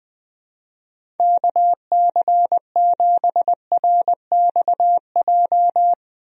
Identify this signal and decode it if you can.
Morse code, heard as a keyed tone, about 21 wpm, 700 Hz. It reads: KC7RXJ
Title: USA Call Signs - 6 Characters - Repeated 3x - 20wpm